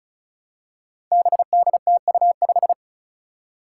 Morse code, heard as a keyed tone, about 36 wpm, 700 Hz. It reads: BDTU5
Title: Five Challenging Letter-Number Combinations 35wpm